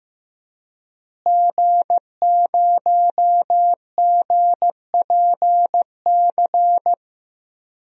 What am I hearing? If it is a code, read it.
Morse code, heard as a keyed tone, about 15 wpm, 700 Hz. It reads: G0GPC
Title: DXCC-100 Call Signs - 5 Characters - Repeated 3x - 15wpm